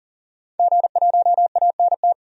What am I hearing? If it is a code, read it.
Morse code, heard as a keyed tone, about 38 wpm, 700 Hz. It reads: G1ANT
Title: DXCC-100 Call Signs - 5 Characters - 40wpm